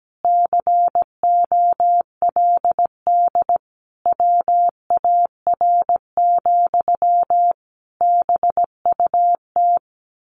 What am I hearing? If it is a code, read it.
Morse code, heard as a keyed tone, about 17 wpm, 700 Hz. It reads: COLD WAR, BUT
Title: Sentences from Top 1200 Words 17wpm